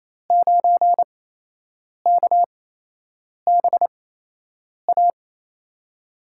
Morse code, heard as a keyed tone, about 28 wpm, spaced as for 8 wpm, 700 Hz. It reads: 9KBA
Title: USA Call Signs - 8x Character Spacing - 28wpm